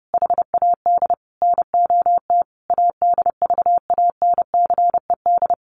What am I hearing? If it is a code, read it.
Morse code, heard as a keyed tone, about 30 wpm, 700 Hz. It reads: HAD NOT ADVANCED
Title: Sentences from Top 1200 Words 30wpm